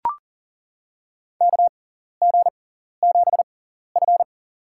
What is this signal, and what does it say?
Morse code, heard as a keyed tone, about 38 wpm, spaced as for 14 wpm, 700 Hz. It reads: KG7F
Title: USA Call Signs - 6x Character Spacing - 40wpm